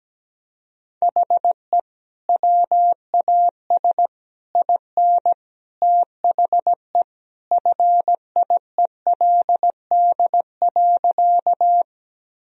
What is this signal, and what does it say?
Morse code, heard as a keyed tone, about 17 wpm, 700 Hz. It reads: HE WAS IN THE FIELD.
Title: Sentences from Top 1000 Words - Review Part 2 - 17wpm